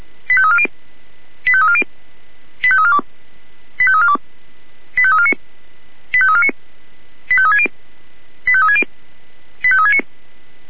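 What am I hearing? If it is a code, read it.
Five-tone selective call (ZVEI), heard as quick runs of five tones, tones of about 70 ms. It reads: 85340 85340 85342 85342 85349 85349 85360 85360 85369
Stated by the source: ZVEI five-tone selective call 85360 received by SDR.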